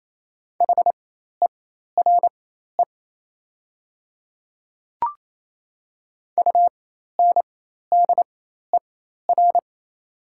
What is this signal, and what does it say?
Morse code, heard as a keyed tone, about 28 wpm, spaced as for 13 wpm, 700 Hz. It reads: HERE UNDER
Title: 4x Character Spacing - Top 200 Words - 28wpm